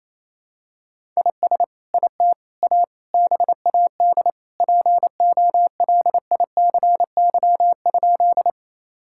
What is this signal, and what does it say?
Morse code, heard as a keyed tone, about 28 wpm, 700 Hz. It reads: IS IT A BAD POLICY?
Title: Sentences from Top 1000 Words - Review Part 5 - 28wpm